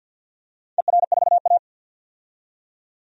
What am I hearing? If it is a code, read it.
Morse code, heard as a keyed tone, about 53 wpm, 700 Hz. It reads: ER4A
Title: DXCC-100 Call Signs - 4 Characters - 50wpm